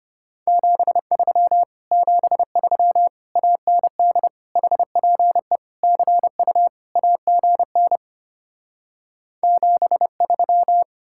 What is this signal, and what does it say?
Morse code, heard as a keyed tone, about 33 wpm, 700 Hz. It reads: 73 73 AND HPE CU AGN 73